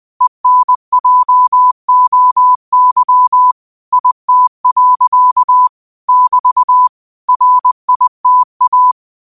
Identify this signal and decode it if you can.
Morse code, heard as a keyed tone, about 20 wpm, 1000 Hz. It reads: ENJOY IT. <BT> RITA
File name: qotdfiles / 20wpm